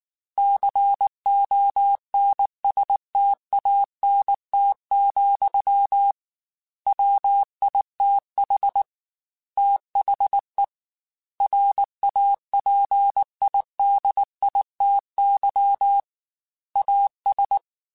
Morse code, heard as a keyed tone, about 19 wpm, 800 Hz. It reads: CONSTANT, WITH THE RAPIDITY AS